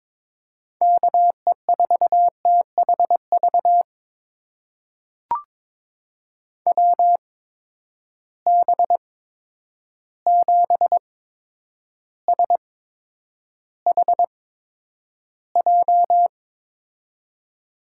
Morse code, heard as a keyed tone, about 22 wpm, spaced as for 9 wpm, 700 Hz. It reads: KE4THV WB7SHJ
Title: USA Call Signs - 6 Characters - 8x Character Spacing - 22wpm